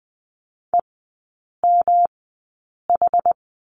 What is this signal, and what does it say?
Morse code, heard as a keyed tone, about 20 wpm, spaced as for 8 wpm, 700 Hz. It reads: EMH
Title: 107 Lesson 18e - Letter m - Three Characters 20wpm